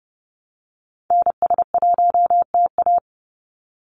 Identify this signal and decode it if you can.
Morse code, heard as a keyed tone, about 30 wpm, 700 Hz. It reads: NS1TA
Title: NA SOTA Activator Callsigns 30wpm